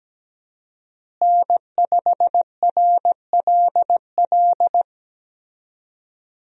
Morse code, heard as a keyed tone, about 17 wpm, 700 Hz. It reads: N5RLL
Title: USA Call Signs - 5 Characters - Repeated 2x - 17wpm